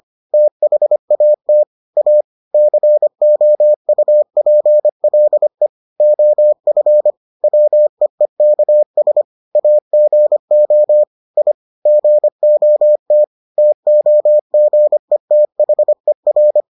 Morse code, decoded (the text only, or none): THAT A COUPLE OF WEEKS AGO I GOT TOGETHER